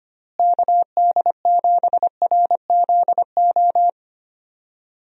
Morse code, heard as a keyed tone, about 25 wpm, 700 Hz. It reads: KD7RZO
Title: USA Call Signs - 6 Characters - Repeated 3x - 25wpm